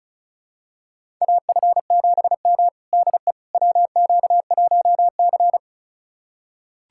Morse code, heard as a keyed tone, about 36 wpm, 700 Hz. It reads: AF7M DE WQ1C
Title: USA Callsign DE Callsign - 4 Characters - 35wpm